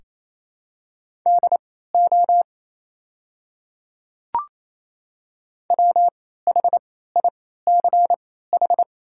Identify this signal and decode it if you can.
Morse code, heard as a keyed tone, about 27 wpm, spaced as for 16 wpm, 700 Hz. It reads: DO WHICH
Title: 3x Character Spacing - Top 200 Words - 28wpm